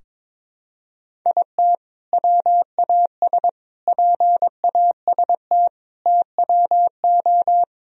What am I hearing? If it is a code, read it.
Morse code, heard as a keyed tone, about 22 wpm, 700 Hz. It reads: IT WAS PAST TWO
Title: Sentences from Top 1000 Words - Review Part 3 - 22wpm